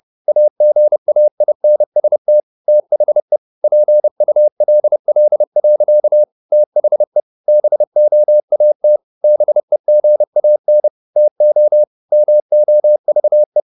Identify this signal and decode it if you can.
Morse code, heard as a keyed tone, about 30 wpm, 600 Hz. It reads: AGAINST THE PULL. THE BOAT BEGAN TO MOVE
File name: the old man and the sea / the old man and the sea 1 30wpm